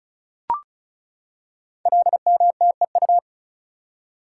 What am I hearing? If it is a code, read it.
Morse code, heard as a keyed tone, about 36 wpm, 690 Hz. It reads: LMTEU
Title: Code Groups - 5 Letters - 35wpm